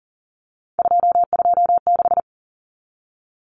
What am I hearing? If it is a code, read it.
Morse code, heard as a keyed tone, about 40 wpm, 700 Hz. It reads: 226
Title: Numbers - 3-Digit 40wpm